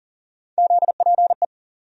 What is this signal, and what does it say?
Morse code, heard as a keyed tone, about 40 wpm, 700 Hz. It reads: ZPE
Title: USA Call Sign Postfixes - 3 Letters 40wpm